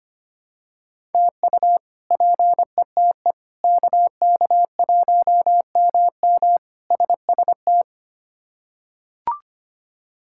Morse code, heard as a keyed tone, about 25 wpm, 700 Hz. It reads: TU PETE KK1MM SST
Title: K1USN SST QSO Practice - 25wpm